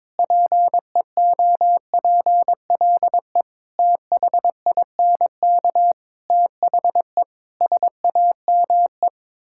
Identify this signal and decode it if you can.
Morse code, heard as a keyed tone, about 22 wpm, 700 Hz. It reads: PEOPLE THINK THE SAME
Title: Sentences from Top 1000 Words - Review Part 5 - 22wpm